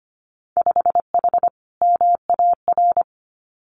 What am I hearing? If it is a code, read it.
Morse code, heard as a keyed tone, about 25 wpm, 700 Hz. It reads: 5H MAR
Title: Winter Field Day Exchanges Encore 25wpm